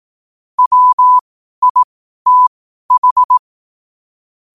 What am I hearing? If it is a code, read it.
Morse code, heard as a keyed tone, about 18 wpm, spaced as for 12 wpm, 1000 Hz. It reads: WITH
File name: qotdfiles / 12wpm